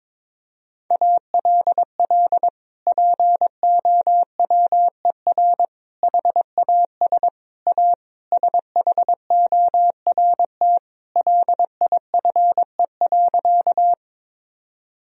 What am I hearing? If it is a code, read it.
Morse code, heard as a keyed tone, about 22 wpm, 700 Hz. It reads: ALL POWER HAS A SHORT LIFE.
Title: Sentences from Top 1000 Words - Review Part 2 - 22wpm